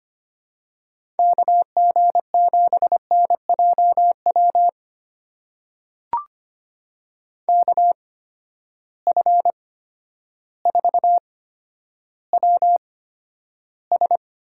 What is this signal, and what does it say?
Morse code, heard as a keyed tone, about 25 wpm, spaced as for 11 wpm, 700 Hz. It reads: KG7NJW KF4WS
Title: USA Call Signs - 6 Characters - 8x Character Spacing - 25wpm